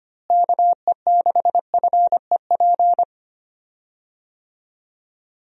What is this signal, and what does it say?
Morse code, heard as a keyed tone, about 25 wpm, 700 Hz. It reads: KE6FEP
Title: USA Call Signs - 6 Characters - Repeated 2x - 25wpm